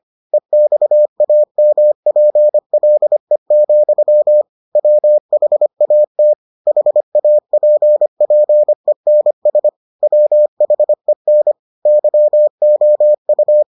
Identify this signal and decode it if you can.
Morse code, heard as a keyed tone, about 25 wpm, 600 Hz. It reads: EXAMPLE, WHAT HAPPENS WHEN YOU